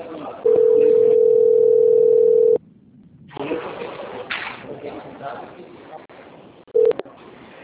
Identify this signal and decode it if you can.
Ringback tone (North America), 440 + 480 Hz, with speech.
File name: out- - - - - .40589